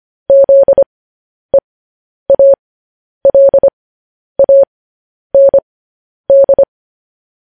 Morse code, heard as a keyed tone, about 25 wpm, spaced as for 10 wpm, 550 Hz. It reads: ZEALAND
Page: news headlines in morse code at 10 wpm